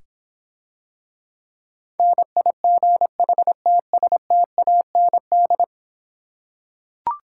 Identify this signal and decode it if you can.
Morse code, heard as a keyed tone, about 26 wpm, 700 Hz. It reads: NIGHTSTAND